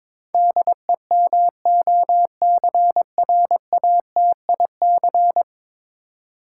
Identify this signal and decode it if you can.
Morse code, heard as a keyed tone, about 22 wpm, 700 Hz. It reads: DEMOCRATIC
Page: morse-code-ninja-practice-22wpm